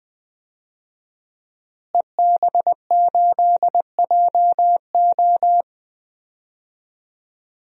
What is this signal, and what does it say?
Morse code, heard as a keyed tone, about 20 wpm, 700 Hz. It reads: EB8JO
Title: DXCC-100 Call Signs - 5 Characters - Repeated 2x - 20wpm